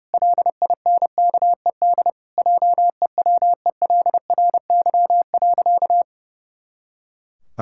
Morse code, heard as a keyed tone, about 30 wpm, 700 Hz. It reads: LINKED JEWELRY.
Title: Pangram Challenge v2 - 30wpm